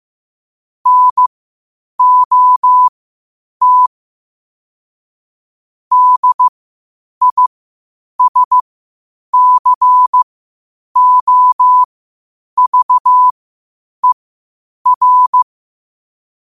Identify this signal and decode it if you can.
Morse code, heard as a keyed tone, about 15 wpm, spaced as for 8 wpm, 1000 Hz. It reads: NOT DISCOVER
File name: qotdfiles / 8wpm